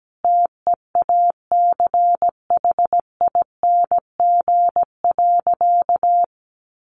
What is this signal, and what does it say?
Morse code, heard as a keyed tone, about 17 wpm, 700 Hz. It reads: TEACHING.
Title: Sentences from Top 1100 Words 17wpm